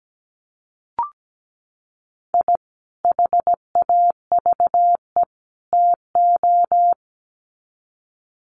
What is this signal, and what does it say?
Morse code, heard as a keyed tone, about 17 wpm, 700 Hz. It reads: I HAVE TO
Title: Sets of 3 Words 17wpm